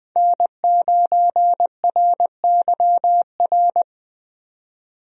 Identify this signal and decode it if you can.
Morse code, heard as a keyed tone, about 20 wpm, 700 Hz. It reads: N9RYR
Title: USA Call Signs - 5 Characters - 20wpm